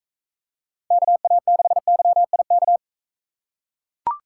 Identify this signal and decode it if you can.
Morse code, heard as a keyed tone, about 43 wpm, 700 Hz. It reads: KA6YIK